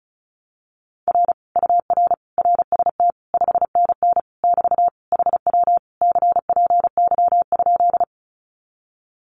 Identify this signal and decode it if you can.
Morse code, heard as a keyed tone, about 36 wpm, 700 Hz. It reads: R UR RST 5NN <BT> HW CPY?
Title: Common QSO Phrases Encore 35wpm